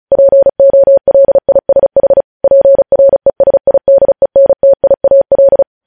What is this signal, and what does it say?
Morse code, heard as a keyed tone, about 36 wpm, 550 Hz. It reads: POLISH PRESIDENTIAL
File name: podcast / News headlines-1x-35x35